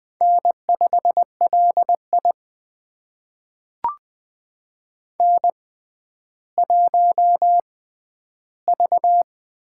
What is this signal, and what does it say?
Morse code, heard as a keyed tone, about 21 wpm, spaced as for 12 wpm, 700 Hz. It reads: N5LI N1V
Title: USA Call Signs - 4 Characters - 6x Character Spacing - 20wpm